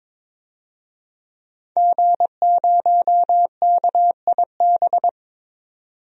Morse code, heard as a keyed tone, about 22 wpm, 700 Hz. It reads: G0KIB
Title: DXCC-100 Call Signs - 5 Characters - 22wpm